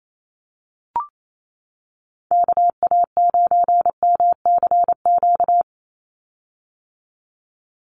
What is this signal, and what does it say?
Morse code, heard as a keyed tone, about 28 wpm, 700 Hz. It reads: KA9MCQ
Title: USA Call Signs 28wpm